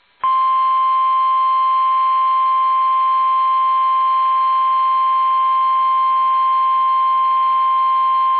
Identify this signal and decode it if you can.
Emergency Alert System (SAME) tones (the 1050 Hz attention tone). It is heard from 1.1 to 7.3 s.
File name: eas / KUSO / KUSO